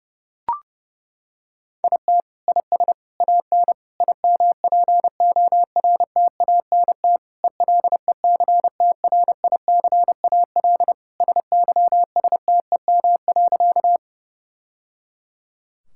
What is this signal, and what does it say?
Morse code, heard as a keyed tone, about 32 wpm, 700 Hz. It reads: IT IS AN IMPORTANT ELECTRICAL SYSTEM.
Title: Sentences from Top 1000 Words - Review Part 6 - 30wpm